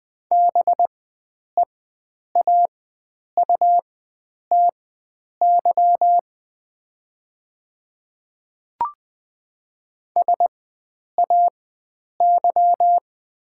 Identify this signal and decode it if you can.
Morse code, heard as a keyed tone, about 21 wpm, spaced as for 9 wpm, 700 Hz. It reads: BEAUTY SAY
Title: 4x Character Spacing - Top 500 Words - 20wpm